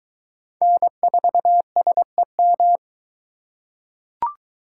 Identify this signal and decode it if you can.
Morse code, heard as a keyed tone, about 22 wpm, 700 Hz. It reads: N4SEM